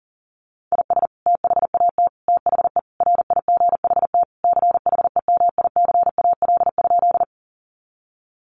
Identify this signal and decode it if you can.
Morse code, heard as a keyed tone, about 40 wpm, 700 Hz. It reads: IS THAT THE RIGHT CHEMICAL?
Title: Sentences from Top 1200 Words 40wpm